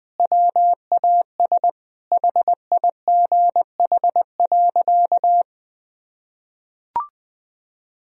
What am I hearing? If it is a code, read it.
Morse code, heard as a keyed tone, about 20 wpm, 700 Hz. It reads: WAS HIGH.
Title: Sentences from Top 1000 Words - Review Part 2 - 20wpm